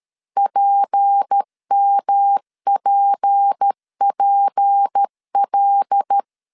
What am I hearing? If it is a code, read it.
Morse code, heard as a keyed tone, about 13 wpm, 800 Hz. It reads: PMPPL
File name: LEZIONI / 6 PKX0 / 6 P 12wpm 2033